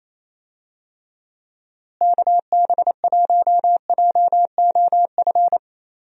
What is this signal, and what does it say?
Morse code, heard as a keyed tone, about 28 wpm, 700 Hz. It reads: KB1JOF
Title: USA Call Signs - 6 Characters - Repeated 2x - 28wpm